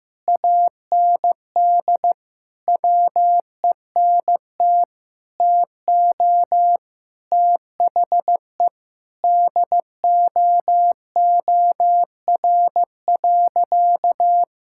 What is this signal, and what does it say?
Morse code, heard as a keyed tone, about 15 wpm, 700 Hz. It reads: AND WENT TO THE DOOR.
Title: Sentences from Top 1600 Words 15wpm